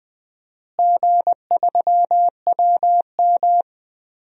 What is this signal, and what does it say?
Morse code, heard as a keyed tone, about 20 wpm, 700 Hz. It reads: G3WM
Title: DXCC-100 Call Signs - 4 Characters - Repeated 2x - 20wpm